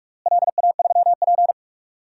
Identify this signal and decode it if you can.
Morse code, heard as a keyed tone, about 41 wpm, 700 Hz. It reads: LA3P
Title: DXCC-100 Call Signs - 4 Characters - Repeated 3x - 45wpm